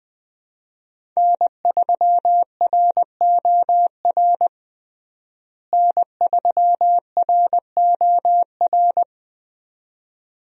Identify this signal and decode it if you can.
Morse code, heard as a keyed tone, about 21 wpm, 700 Hz. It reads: N3ROR N3ROR
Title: USA Call Signs - 5 Characters - Repeated 2x - 20wpm